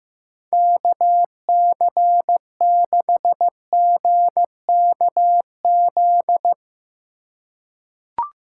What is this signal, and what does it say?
Morse code, heard as a keyed tone, about 15 wpm, 700 Hz. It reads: KC6GKZ
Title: USA Call Signs - 4x Character Spacing - 15wpm